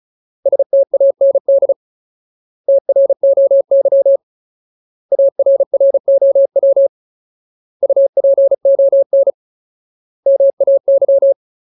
Morse code, heard as a keyed tone, about 34 wpm, 550 Hz. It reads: STAND TROY ARROW UPON MAY
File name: Ultimate Word List / 35WPM Ultimate